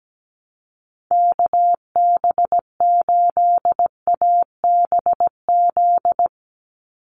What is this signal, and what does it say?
Morse code, heard as a keyed tone, about 17 wpm, 700 Hz. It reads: KB8ABZ
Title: USA Call Signs 17wpm